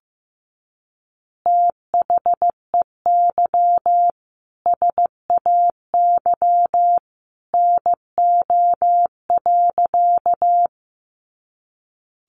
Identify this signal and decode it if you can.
Morse code, heard as a keyed tone, about 15 wpm, 700 Hz. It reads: THEY SAY NO.
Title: Easy Phrases 15wpm